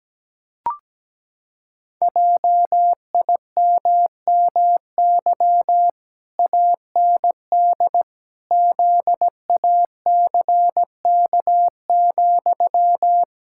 Morse code, heard as a keyed tone, about 17 wpm, 700 Hz. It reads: JIMMY AND ZACK,
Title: Pangram Challenge v2 - 17wpm